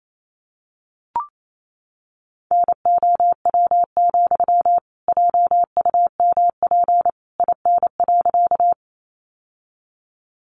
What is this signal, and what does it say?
Morse code, heard as a keyed tone, about 28 wpm, 700 Hz. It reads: NOW, JUMP IN.
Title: Sentences from Top 1300 Words 28wpm